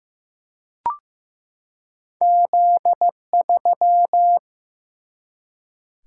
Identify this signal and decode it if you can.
Morse code, heard as a keyed tone, about 15 wpm, 700 Hz. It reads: Z3
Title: Single-Letter Single-Number 15wpm